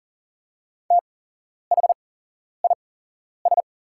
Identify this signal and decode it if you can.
Morse code, heard as a keyed tone, about 40 wpm, spaced as for 11 wpm, 690 Hz. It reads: THIS
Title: 8x Character Spacing - Top 100 Words - 40wpm